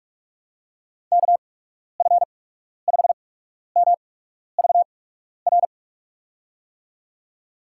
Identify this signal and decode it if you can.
Morse code, heard as a keyed tone, about 46 wpm, spaced as for 12 wpm, 700 Hz. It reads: KF5MVR
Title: USA Call Signs - 6 Characters - 8x Character Spacing - 45wpm